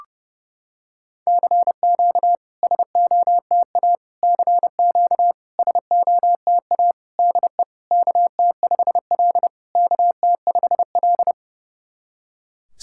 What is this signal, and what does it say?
Morse code, heard as a keyed tone, about 30 wpm, 700 Hz. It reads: CQ SOTA CQ SOTA DE KT5L KT5L
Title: SOTA QSO Practice 30wpm